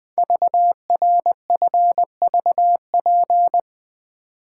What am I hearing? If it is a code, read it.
Morse code, heard as a keyed tone, about 21 wpm, 700 Hz. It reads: VRFVP
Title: Code Groups - 5 Letters - 20wpm